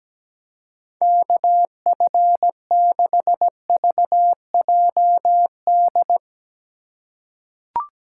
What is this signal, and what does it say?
Morse code, heard as a keyed tone, about 17 wpm, 700 Hz. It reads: KF6VJD
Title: USA Call Signs - 6 Characters - 3x Character Spacing - 17wpm